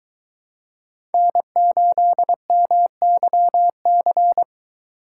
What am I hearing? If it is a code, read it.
Morse code, heard as a keyed tone, about 23 wpm, 700 Hz. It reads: N8MYC